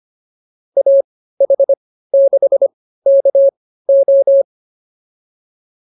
Morse code, heard as a keyed tone, about 25 wpm, spaced as for 15 wpm, 550 Hz. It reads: AH6KO